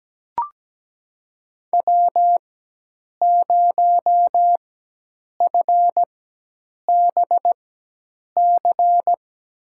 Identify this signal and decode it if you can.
Morse code, heard as a keyed tone, about 17 wpm, spaced as for 8 wpm, 700 Hz. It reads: W0FBC
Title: USA Call Signs - 4x Character Spacing - 17wpm